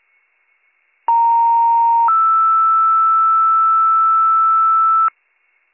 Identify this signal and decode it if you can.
Two-tone page, heard as a one-second tone, then a three-second tone, about 930 then 1320 Hz, with noise. It is heard from 1.1 to 5.1 s.